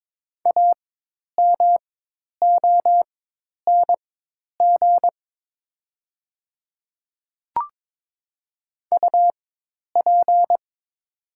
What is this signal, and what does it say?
Morse code, heard as a keyed tone, about 22 wpm, spaced as for 10 wpm, 700 Hz. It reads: AMONG UP
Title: 4x Character Spacing - Top 500 Words - 22wpm